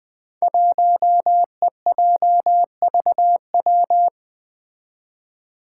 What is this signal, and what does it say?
Morse code, heard as a keyed tone, about 20 wpm, 700 Hz. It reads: 1EJVW
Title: Five Challenging Letter-Number Combinations 20wpm